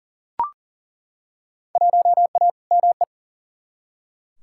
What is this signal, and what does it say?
Morse code, heard as a keyed tone, about 40 wpm, 700 Hz. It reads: 1A ME
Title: Field Day Exchanges Encore - 40wpm